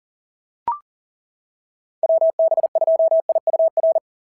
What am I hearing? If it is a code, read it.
Morse code, heard as a keyed tone, about 40 wpm, 660 Hz. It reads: WB2IUR
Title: SOTA QSO Practice 40wpm